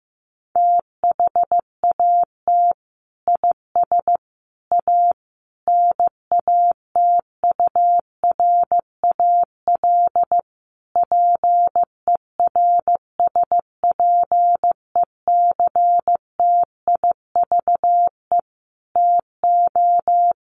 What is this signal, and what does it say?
Morse code, heard as a keyed tone, about 15 wpm, 700 Hz. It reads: THAT IS A NATURAL PERSPECTIVE TO
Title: Sentences from Top 1200 Words 15wpm